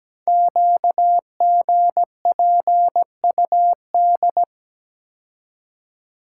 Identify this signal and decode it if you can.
Morse code, heard as a keyed tone, about 17 wpm, 700 Hz. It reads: QGPUD
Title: Code Groups - 5 Letters - 17wpm